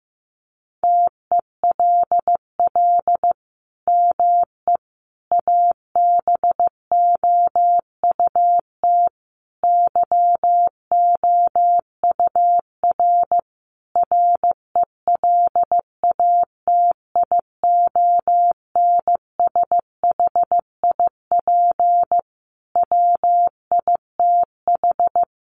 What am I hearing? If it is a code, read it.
Morse code, heard as a keyed tone, about 14 wpm, 700 Hz. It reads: TELL ME ABOUT YOUR RELATIONSHIP WITH
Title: Sentences from Top 1100 Words 15wpm